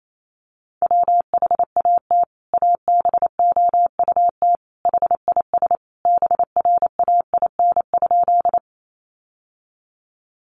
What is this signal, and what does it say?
Morse code, heard as a keyed tone, about 28 wpm, 700 Hz. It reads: WHAT ABOUT HIS BRAIN?
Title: Sentences from Top 1100 Words 28wpm